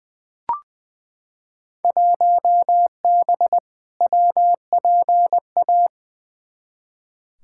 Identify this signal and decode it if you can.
Morse code, heard as a keyed tone, about 20 wpm, 700 Hz. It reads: 1B WPA
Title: Field Day Exchanges - 20wpm